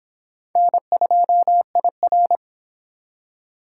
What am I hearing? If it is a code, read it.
Morse code, heard as a keyed tone, about 26 wpm, 700 Hz. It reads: N2IR